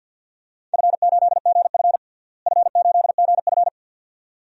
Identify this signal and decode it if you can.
Morse code, heard as a keyed tone, about 47 wpm, 700 Hz. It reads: F8GF F8GF
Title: DXCC-100 Call Signs - 4 Characters - Repeated 2x - 50wpm